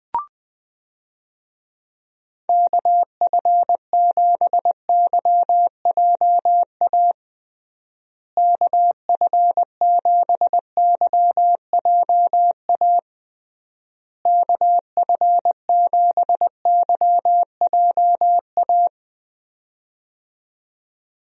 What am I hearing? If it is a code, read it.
Morse code, heard as a keyed tone, about 20 wpm, 700 Hz. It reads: KF7YJA KF7YJA KF7YJA
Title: USA Call Signs - 6 Characters - Repeated 3x - 20wpm